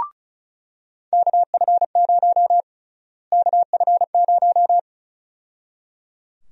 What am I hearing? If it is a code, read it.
Morse code, heard as a keyed tone, about 34 wpm, 700 Hz. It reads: KF0 KF0
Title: USA Call Sign Prefixes - Repeated 2x - 35wpm